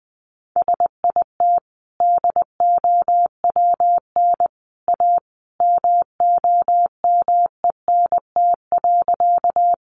Morse code, heard as a keyed tone, about 20 wpm, 700 Hz. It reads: SIT DOWN A MOMENT.
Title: Sentences from Top 1000 Words 20wpm